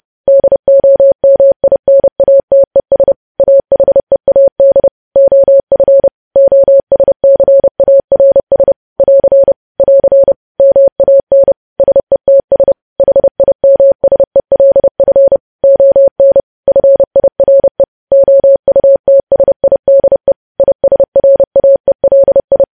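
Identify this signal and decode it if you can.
Morse code, heard as a keyed tone, about 30 wpm, 550 Hz. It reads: DOMINATES AHEAD OF OSCARS <AR> <AR> MAN SETS HIMSELF ON FIRE OUTSIDE ISRAELI
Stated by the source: Morse news headlines sent at 30 wpm, opening with vvv.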